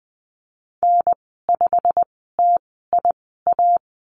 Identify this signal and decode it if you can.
Morse code, heard as a keyed tone, about 20 wpm, spaced as for 15 wpm, 700 Hz. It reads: N5TIA
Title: 088 Lesson 15g - Number 5 - Call Signs 20wpm